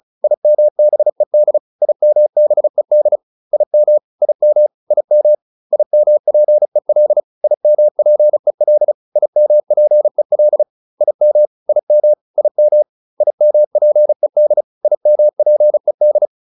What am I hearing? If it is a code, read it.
Morse code, heard as a keyed tone, about 34 wpm, 600 Hz. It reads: IMBED IMBED IM IM IM IMPEL IMPEL IMPEL IM IM IM IMPED IMPED